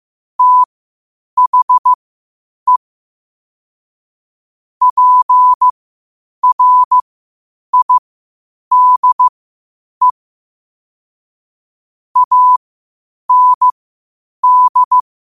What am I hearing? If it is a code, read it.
Morse code, heard as a keyed tone, about 15 wpm, spaced as for 8 wpm, 1000 Hz. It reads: THE PRIDE AND
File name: qotdfiles / 8wpm